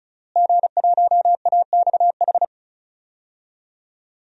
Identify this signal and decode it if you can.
Morse code, heard as a keyed tone, about 34 wpm, 700 Hz. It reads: G1AXH
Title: DXCC-100 Call Signs - 5 Characters - Repeated 2x - 35wpm